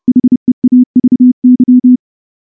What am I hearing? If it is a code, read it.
Morse code, heard as a keyed tone, about 30 wpm, 250 Hz. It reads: HEAVY
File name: morse / morse-training-30WPM